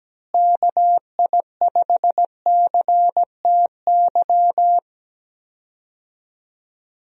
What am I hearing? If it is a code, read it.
Morse code, heard as a keyed tone, about 17 wpm, 700 Hz. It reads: KI5CTY
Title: USA Call Signs - 6 Characters - Repeated 3x - 17wpm